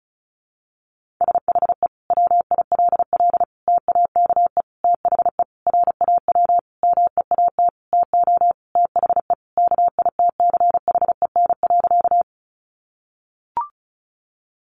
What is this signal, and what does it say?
Morse code, heard as a keyed tone, about 34 wpm, 700 Hz. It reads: SHE WILL TAKE THE RAW MEAT TO THE KITCHEN.
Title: Sentences from Top 1500 Words 35wpm